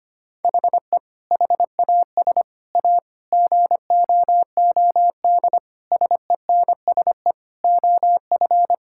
Morse code, heard as a keyed tone, about 25 wpm, 700 Hz. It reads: HE HAS A GOOD SENSE OF
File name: practice / 25wpm / Sentences with 2-Word Phrases - 25wpm